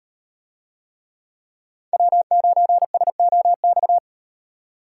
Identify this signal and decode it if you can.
Morse code, heard as a keyed tone, about 38 wpm, 700 Hz. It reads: W9SOX